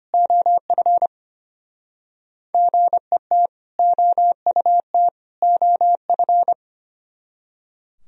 Morse code, heard as a keyed tone, about 30 wpm, 700 Hz. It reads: OF GET OUT OF
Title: Speed-Racing - Sets of 3 Words 25wpm